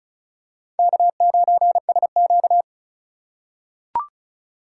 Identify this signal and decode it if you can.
Morse code, heard as a keyed tone, about 36 wpm, 700 Hz. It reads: K9SQ
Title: USA Call Signs - 4 Characters - 4x Character Spacing - 35wpm